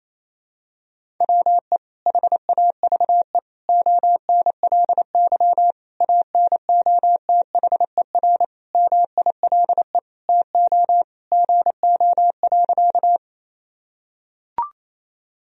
Morse code, heard as a keyed tone, about 28 wpm, 700 Hz. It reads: WE HAVE ONLY ANOTHER MILE TO GO.
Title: Sentences from Top 1000 Words - Review Part 1 - 28wpm